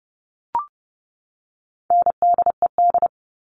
Morse code, heard as a keyed tone, about 30 wpm, 700 Hz. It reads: NDED
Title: All Cut Numbers - 4-Digit 30wpm